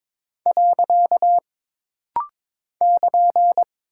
Morse code, heard as a keyed tone, about 22 wpm, 700 Hz. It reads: . <KN>
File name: practice / 22wpm / Single Punctuation-Prosign - Rapid-Fire - 22wpm